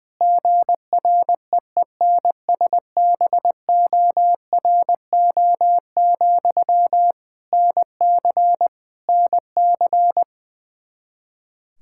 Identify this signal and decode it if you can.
Morse code, heard as a keyed tone, about 20 wpm, 700 Hz. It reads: GREENSBORO, NC NC
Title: K1USN SST QSO Practice 20wpm